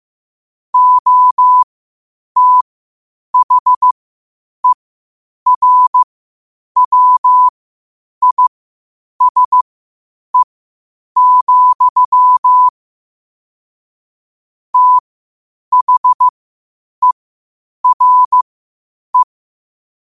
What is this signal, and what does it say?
Morse code, heard as a keyed tone, about 15 wpm, spaced as for 8 wpm, 1000 Hz. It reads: OTHERWISE, THERE
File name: qotdfiles / 8wpm